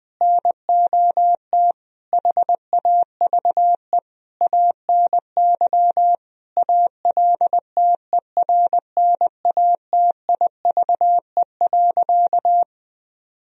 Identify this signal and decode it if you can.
Morse code, heard as a keyed tone, about 20 wpm, 700 Hz. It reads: NOT HAVE ANY ALTERNATIVE.
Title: Sentences from Top 1000 Words - Review Part 4 - 20wpm